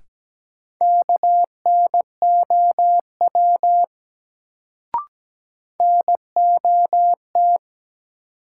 Morse code, heard as a keyed tone, about 17 wpm, 700 Hz. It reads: KNOW NOT
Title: Sets of 2 Words - New Word Drill - 17wpm